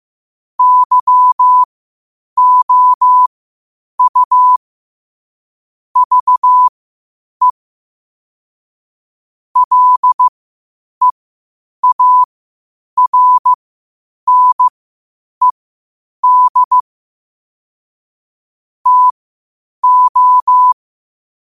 Morse code, heard as a keyed tone, about 15 wpm, spaced as for 8 wpm, 1000 Hz. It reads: YOU VE LEARNED TO
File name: qotdfiles / 8wpm